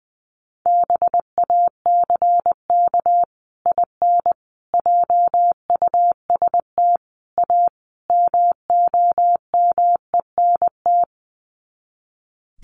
Morse code, heard as a keyed tone, about 20 wpm, 700 Hz. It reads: BACK IN JUST A MOMENT
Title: Sets of 5 Words Encore 20wpm